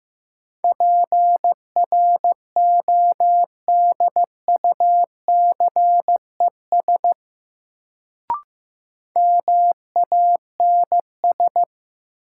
Morse code, heard as a keyed tone, about 15 wpm, 700 Hz. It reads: PRODUCES MANS
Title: Sentences from Top 1000 Words - Review Part 6 - New Word Drill - 15wpm